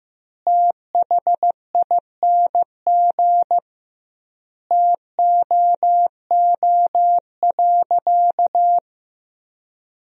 Morse code, heard as a keyed tone, about 15 wpm, 700 Hz. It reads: THING TOO.
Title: 2x Word-Spacing - Sentences from Top 500 Words - 15wpm